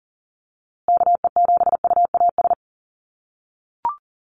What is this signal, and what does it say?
Morse code, heard as a keyed tone, about 40 wpm, 700 Hz. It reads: KE7UAS
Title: USA Call Signs Repeated 3x 40wpm